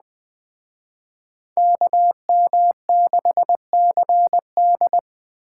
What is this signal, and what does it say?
Morse code, heard as a keyed tone, about 20 wpm, 700 Hz. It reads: KM6CD
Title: Speed Racing - USA 2x2 Call Signs 17wpm